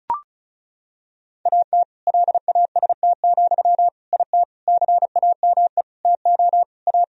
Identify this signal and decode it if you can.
Morse code, heard as a keyed tone, about 36 wpm, 700 Hz. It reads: AT LAST, IT CAME TO A
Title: Sentences from Top 1000 Words - Review Part 1 - 35wpm